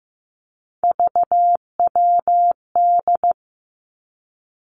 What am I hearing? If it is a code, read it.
Morse code, heard as a keyed tone, about 15 wpm, 700 Hz. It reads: VWD
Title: Three Challenging Letter-Number Combinations 15wpm